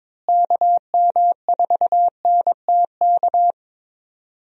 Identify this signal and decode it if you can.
Morse code, heard as a keyed tone, about 22 wpm, 700 Hz. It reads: KM4NTK
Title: USA Call Signs - 6 Characters - Repeated 2x - 22wpm